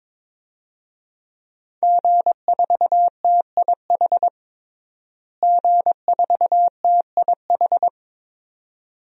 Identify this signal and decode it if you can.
Morse code, heard as a keyed tone, about 22 wpm, 700 Hz. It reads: G4TIH G4TIH
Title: DXCC-100 Call Signs - 5 Characters - Repeated 2x - 22wpm